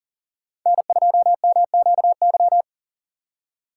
Morse code, heard as a keyed tone, about 39 wpm, 700 Hz. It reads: N2MQY
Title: USA Call Signs - 3x Character Spacing - 40wpm